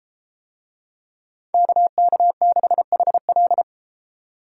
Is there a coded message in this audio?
Yes: Morse code from 1.5 to 3.6 s, heard as a keyed tone, about 32 wpm, 700 Hz. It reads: KK6HL